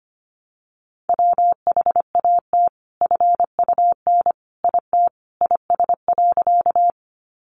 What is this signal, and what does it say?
Morse code, heard as a keyed tone, about 25 wpm, 700 Hz. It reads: WHAT FUN IT IS.
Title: Sentences from Top 700 Words 25wpm